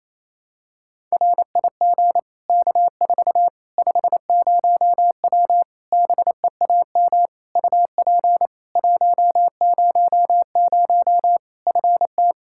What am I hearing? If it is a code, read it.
Morse code, heard as a keyed tone, about 28 wpm, 700 Hz. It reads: RIG K4 50W BEAM UP 100 FT
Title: Practice QSO - Exchange 2 - Info - 28wpm